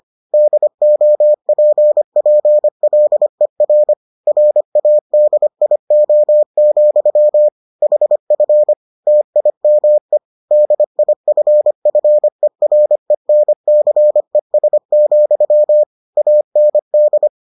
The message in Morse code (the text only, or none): DOPPLER RADIO, HF TIME DIFFERENCES, AND